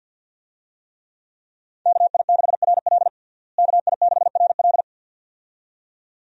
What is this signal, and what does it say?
Morse code, heard as a keyed tone, about 49 wpm, 700 Hz. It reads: KI6RL KI6RL
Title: USA Call Signs - 5 Characters - Repeated 2x - 50wpm